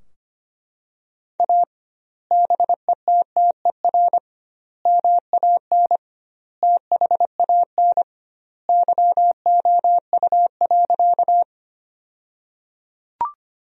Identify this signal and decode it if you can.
Morse code, heard as a keyed tone, about 25 wpm, 700 Hz. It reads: A BETTER MAN THAN YOU.
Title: 2x Word-Spacing - Sentences from Top 400 Words - 25wpm